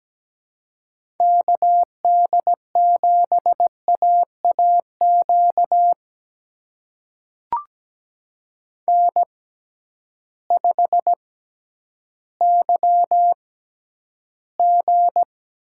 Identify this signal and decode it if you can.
Morse code, heard as a keyed tone, about 17 wpm, spaced as for 10 wpm, 700 Hz. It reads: KD7AAQ N5YG
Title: USA Call Signs - 6x Character Spacing - 17wpm